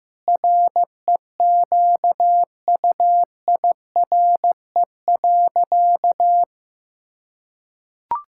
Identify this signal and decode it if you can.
Morse code, heard as a keyed tone, about 15 wpm, 700 Hz. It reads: REQUIRE.
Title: Sentences from Top 1000 Words - Review Part 6 - 15wpm